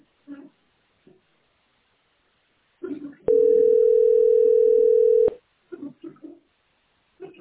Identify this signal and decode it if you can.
Ringback tone (North America), 440 + 480 Hz.